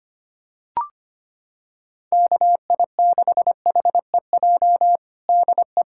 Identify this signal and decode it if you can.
Morse code, heard as a keyed tone, about 25 wpm, 700 Hz. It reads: KI6HEJ DE
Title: USA Callsign DE Callsign 25wpm